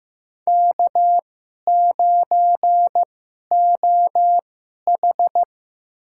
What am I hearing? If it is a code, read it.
Morse code, heard as a keyed tone, about 15 wpm, spaced as for 11 wpm, 700 Hz. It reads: K9OH
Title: USA Call Signs - 4 Characters - 2x Character Spacing - 15wpm